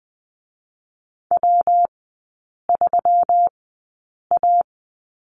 Morse code, heard as a keyed tone, about 20 wpm, spaced as for 8 wpm, 700 Hz. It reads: W3A
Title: 184 Lesson 30e - Letter b - Three Characters 20wpm